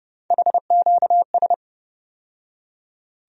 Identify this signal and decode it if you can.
Morse code, heard as a keyed tone, about 31 wpm, 700 Hz. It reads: HQS
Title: USA Call Sign Postfixes - 3 Letters 30wpm